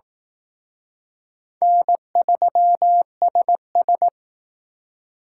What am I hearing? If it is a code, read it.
Morse code, heard as a keyed tone, about 19 wpm, 700 Hz. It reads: N3SS